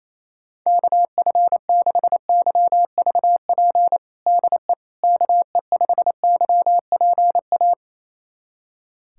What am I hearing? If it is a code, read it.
Morse code, heard as a keyed tone, about 28 wpm, 700 Hz. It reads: KF6YVP DE KE5YPA
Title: USA Callsign DE Callsign 28wpm